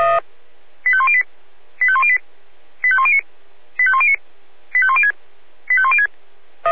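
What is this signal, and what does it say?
Five-tone selective call (ZVEI), heard as quick runs of five tones, tones of about 70 ms. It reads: 85108 85108 85109 85109 85106 85106
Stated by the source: ZVEI five-tone selective call 85101 received on SDR.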